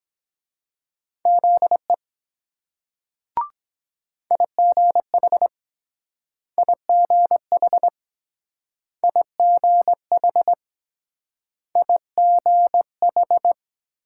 Morse code, heard as a keyed tone, about 22 wpm, 700 Hz. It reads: ZE IGH IGH IGH IGH